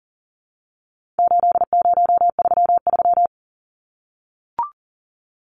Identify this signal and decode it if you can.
Morse code, heard as a keyed tone, about 40 wpm, 700 Hz. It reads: 8033
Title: Numbers - 4-Digit 40wpm